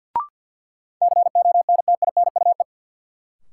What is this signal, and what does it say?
Morse code, heard as a keyed tone, about 53 wpm, 700 Hz. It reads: CONTINUE
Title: Sentences from Top 1000 Words - Review Part 6 - New Word Drill - 50wpm